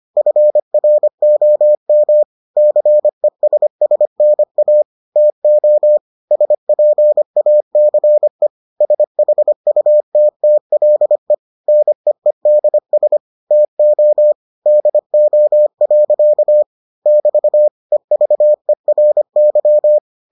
Morse code, heard as a keyed tone, about 25 wpm, 600 Hz. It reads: FROM CESSNA TO SPACE SHUTTLE NEEDS TO DO. <BT> EVERY